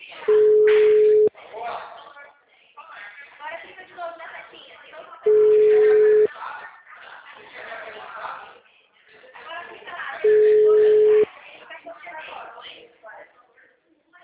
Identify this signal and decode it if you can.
Ringback tone (Europe, 425 Hz), with speech.
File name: out- -2017- - - .112